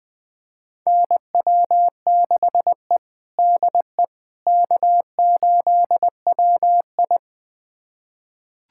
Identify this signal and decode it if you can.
Morse code, heard as a keyed tone, about 21 wpm, 700 Hz. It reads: NW6E DE K8WI
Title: USA Callsign DE Callsign - 4 Characters - 20wpm